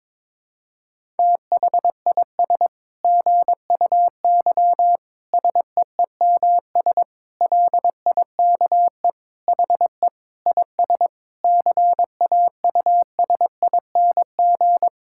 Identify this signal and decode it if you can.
Morse code, heard as a keyed tone, about 22 wpm, 700 Hz. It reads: THIS GUY SEEMS LIKE HE IS CAUSING
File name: Sentences with 9-Word Phrases - 22wpm